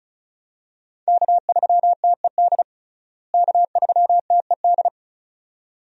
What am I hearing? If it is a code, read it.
Morse code, heard as a keyed tone, about 34 wpm, 700 Hz. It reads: K3TED K3TED
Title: USA Call Signs - 5 Characters - Repeated 2x - 35wpm